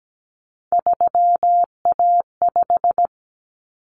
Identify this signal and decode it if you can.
Morse code, heard as a keyed tone, about 17 wpm, 690 Hz. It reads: 3A5
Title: Common Cut Numbers - 3-Digit 17wpm